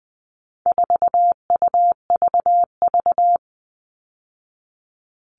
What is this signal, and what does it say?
Morse code, heard as a keyed tone, about 20 wpm, 700 Hz. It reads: 4UVV
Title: All Cut Numbers - 4-Digit 20wpm